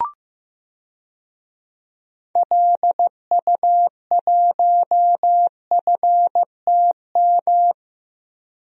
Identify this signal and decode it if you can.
Morse code, heard as a keyed tone, about 15 wpm, 700 Hz. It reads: LU1FTM
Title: DXCC-100 Call Signs - 6 Characters - Repeated 3x - 15wpm